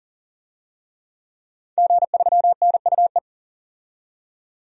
Morse code, heard as a keyed tone, about 38 wpm, 700 Hz. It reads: G3NUE
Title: DXCC-100 Call Signs - 5 Characters - 40wpm